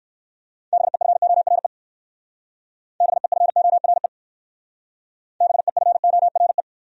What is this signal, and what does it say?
Morse code, heard as a keyed tone, about 61 wpm, 700 Hz. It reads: BEFORE BEFORE BEFORE